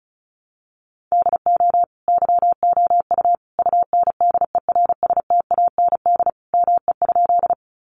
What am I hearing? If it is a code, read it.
Morse code, heard as a keyed tone, about 34 wpm, 700 Hz. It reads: DO YOU UNDERSTAND ME?
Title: Sentences from Top 600 Words 35wpm